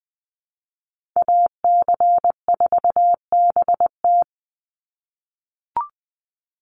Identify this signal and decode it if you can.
Morse code, heard as a keyed tone, about 20 wpm, 700 Hz. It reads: AC4BT
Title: Call Signs - 20wpm